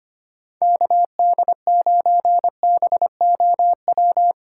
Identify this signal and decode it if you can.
Morse code, heard as a keyed tone, about 25 wpm, 700 Hz. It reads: KD9BOW
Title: USA Call Signs - 6 Characters - 4x Character Spacing - 25wpm